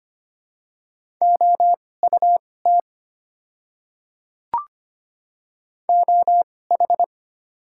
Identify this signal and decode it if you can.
Morse code, heard as a keyed tone, about 25 wpm, spaced as for 18 wpm, 690 Hz. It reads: OUT OH
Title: 2x Character Spacing - Top 500 Words - 25wpm